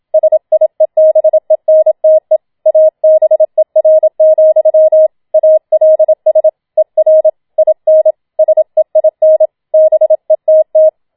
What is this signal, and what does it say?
Morse code, heard as a keyed tone, about 27 wpm, 630 Hz. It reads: SIEBENTE ABER, ALS ER IN SEIN BETT
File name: Schneewittchen Teil2 27wpm